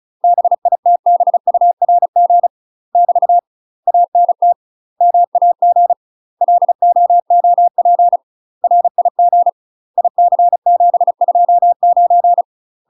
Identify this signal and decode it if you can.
Morse code, heard as a keyed tone, about 36 wpm, 700 Hz. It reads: BITBURG <BT> ANT MAG LOOP RIG IC729
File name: UK MASTER QSOs TRAINING / 35WPM / UK MASTER QSOs 0041